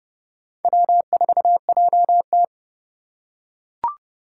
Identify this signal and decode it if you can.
Morse code, heard as a keyed tone, about 29 wpm, 700 Hz. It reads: W4JT